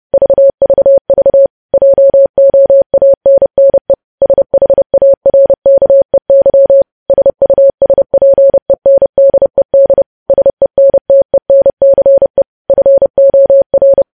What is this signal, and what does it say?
Morse code, heard as a keyed tone, about 30 wpm, 550 Hz. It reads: VVV JOANNE SHARKEY SUSPENDED SENTENCE FOR
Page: News Headlines in Morse Code at 30 WPM